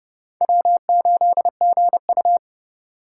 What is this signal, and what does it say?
Morse code, heard as a keyed tone, about 30 wpm, 700 Hz. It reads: W8GU
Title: K1USN SST QSO Practice - 30wpm